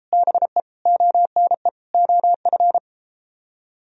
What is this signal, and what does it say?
Morse code, heard as a keyed tone, about 31 wpm, 700 Hz. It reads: BE ONE OF